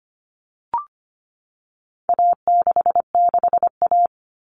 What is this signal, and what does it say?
Morse code, heard as a keyed tone, about 25 wpm, 700 Hz. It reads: A66A
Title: All Cut Numbers - 4-Digit 25wpm